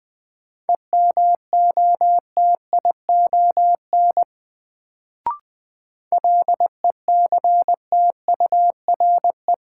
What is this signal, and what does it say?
Morse code, heard as a keyed tone, about 21 wpm, 700 Hz. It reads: EMOTION LECTURE
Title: Sentences from Top 1600 Words - New Word Drill - 20wpm